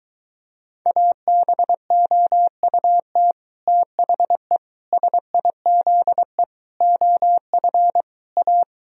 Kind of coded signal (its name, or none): Morse code